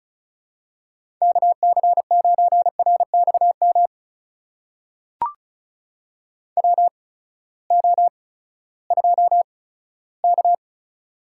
Morse code, heard as a keyed tone, about 36 wpm, spaced as for 18 wpm, 700 Hz. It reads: KC9RXM WO2K
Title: USA Call Signs - 8x Character Spacing - 35wpm